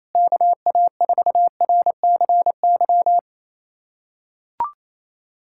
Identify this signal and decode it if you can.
Morse code, heard as a keyed tone, about 28 wpm, 700 Hz. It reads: KA4RCY
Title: USA Callsign DE Callsign 28wpm